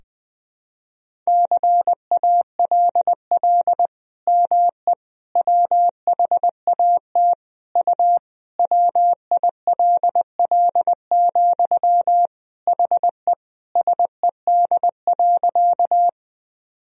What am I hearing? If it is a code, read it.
Morse code, heard as a keyed tone, about 20 wpm, 700 Hz. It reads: CALL ME WHAT U WILL, HE SED.
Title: Sentences from Top 200 Words and Abbreviations 20wpm